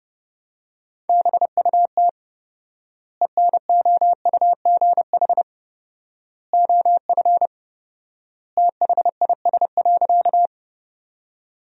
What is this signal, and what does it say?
Morse code, heard as a keyed tone, about 32 wpm, 700 Hz. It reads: BUT ENOUGH OF THIS.
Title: 4x Word-Spacing - Sentences from Top 400 Words - 30wpm